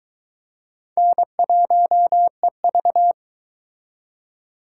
Morse code, heard as a keyed tone, about 22 wpm, 700 Hz. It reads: N1EV